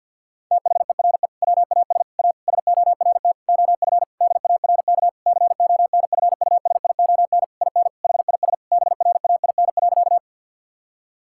Morse code, heard as a keyed tone, about 50 wpm, 700 Hz. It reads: THERE WAS A SORT OF DARK CONFUSION IN HIS BRAIN.
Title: Sentences from Top 1600 Words 50wpm